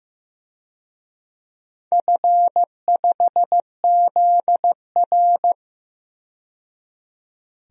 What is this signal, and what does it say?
Morse code, heard as a keyed tone, about 15 wpm, 700 Hz. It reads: F5ZR
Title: DXCC-100 Call Signs - 4 Characters - 15wpm